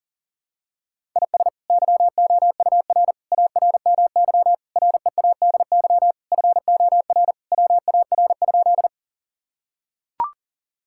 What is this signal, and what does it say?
Morse code, heard as a keyed tone, about 40 wpm, 700 Hz. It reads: IS YOUR ARMY READY FOR WAR?
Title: Sentences from Top 1000 Words - Review Part 5 - 40wpm